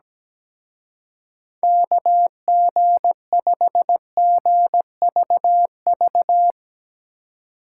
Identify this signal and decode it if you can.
Morse code, heard as a keyed tone, about 19 wpm, 700 Hz. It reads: KG5GVV
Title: Speed Racing - USA 2x3 Call Signs 17wpm